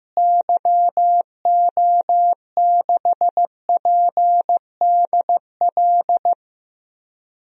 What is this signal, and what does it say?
Morse code, heard as a keyed tone, about 15 wpm, 700 Hz. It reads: YO6PDL
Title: DXCC-100 Call Signs - 6 Characters - Repeated 2x - 15wpm